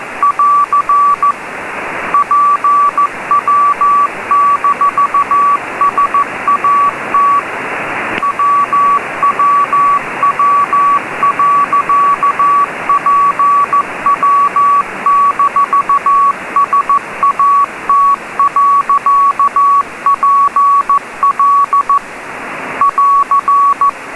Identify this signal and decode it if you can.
Morse code, heard as a keyed tone, about 14 wpm, 1140 Hz, with noise. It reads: <AR> PW-SAT WWW.PW-SAT.PL <AR>
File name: pwsat 04h58 UTC CW DK3WN